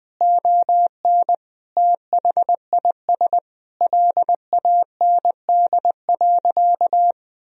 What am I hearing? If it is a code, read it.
Morse code, heard as a keyed tone, about 20 wpm, 700 Hz. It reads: ON THIS LAND.
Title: Sentences from Top 1000 Words - Review Part 4 - 20wpm